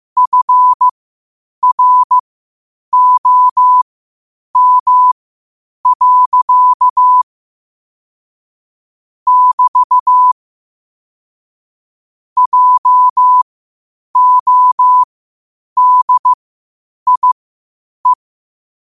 Morse code, heard as a keyed tone, about 15 wpm, spaced as for 8 wpm, 1000 Hz. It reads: FROM. <BT> JODIE
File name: qotdfiles / 8wpm